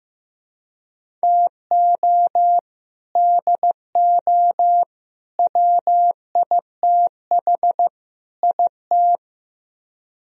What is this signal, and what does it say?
Morse code, heard as a keyed tone, about 17 wpm, 700 Hz. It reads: TO DO WITH IT
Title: Speed-Racing - Sets of 4 Words 15wpm